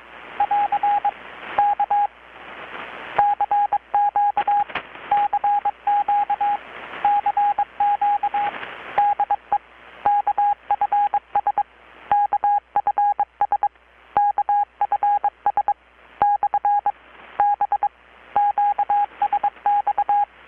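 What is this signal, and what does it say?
Morse code, heard as a keyed tone, about 23 wpm, 820 Hz, with radio noise. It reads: <AR> K CQ CQ CQ DE KFS KFS KFS / B QSX
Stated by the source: Morse from KPH's Night of Nights XX maritime radiotelegraphy event, received in Chiapas, Mexico about 3,800 km away on a Kenwood R-600 with a grounded 10 m long-wire antenna.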